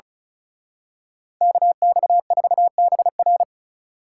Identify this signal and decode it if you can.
Morse code, heard as a keyed tone, about 33 wpm, 700 Hz. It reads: KX4BR
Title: Speed Racing - USA Call Signs 35wpm